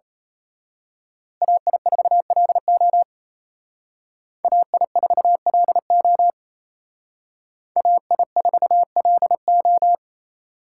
Morse code, heard as a keyed tone, about 31 wpm, 700 Hz. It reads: AI4LO AI4LO AI4LO
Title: Speed Racing - USA Call Signs 28wpm